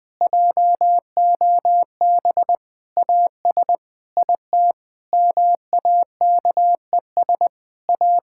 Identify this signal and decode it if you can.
Morse code, heard as a keyed tone, about 19 wpm, 700 Hz. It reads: JOB AS IT MAKES A
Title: Difficult Sentences with Hints - 20wpm